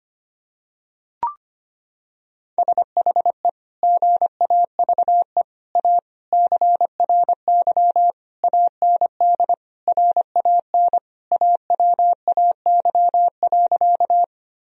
Morse code, heard as a keyed tone, about 25 wpm, 700 Hz. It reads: SHE GAVE A CRY AND RAN AWAY.
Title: Sentences from Top 1000 Words - Review Part 3 - 25wpm